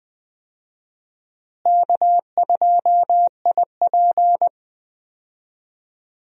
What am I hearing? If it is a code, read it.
Morse code, heard as a keyed tone, about 20 wpm, 700 Hz. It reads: K2IP
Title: USA Call Signs - 4 Characters - Repeated 3x - 20wpm